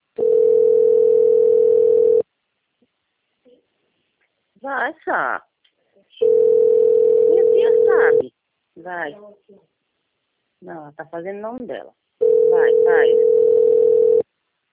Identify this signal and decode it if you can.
Ringback tone (North America), 440 + 480 Hz, with speech.
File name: rg-1600-unknown- - - .1622